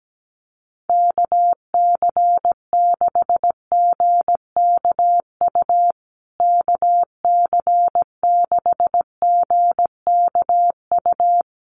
Morse code, heard as a keyed tone, about 17 wpm, 700 Hz. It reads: KC6GKU KC6GKU
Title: USA Call Signs Repeated 3x 17wpm